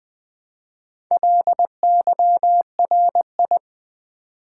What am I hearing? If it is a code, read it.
Morse code, heard as a keyed tone, about 20 wpm, 700 Hz. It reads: LYRI
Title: Build A Word - 1 Letter At A Time - 5-Letter Words - 20wpm